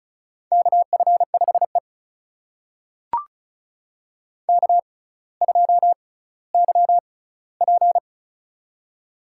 Morse code, heard as a keyed tone, about 36 wpm, spaced as for 18 wpm, 700 Hz. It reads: KF5E K2YP
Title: USA Call Signs - 4 Characters - 6x Character Spacing - 35wpm